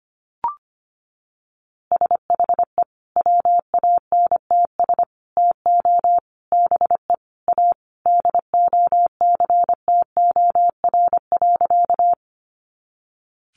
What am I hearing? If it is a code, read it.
Morse code, heard as a keyed tone, about 25 wpm, 700 Hz. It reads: SHE WANTS TO BE A DOCTOR.
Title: Easy Phrases 25wpm